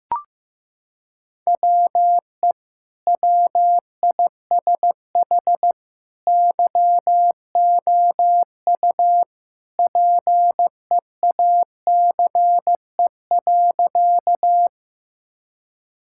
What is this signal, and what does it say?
Morse code, heard as a keyed tone, about 15 wpm, 700 Hz. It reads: WE WISH YOU PEACE.
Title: Sentences from Top 1600 Words 15wpm